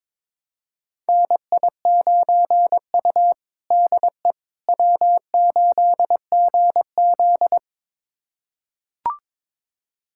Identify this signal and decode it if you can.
Morse code, heard as a keyed tone, about 22 wpm, 700 Hz. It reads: NI9U DE W8GZ
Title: USA Callsign DE Callsign - 4 Characters - 22wpm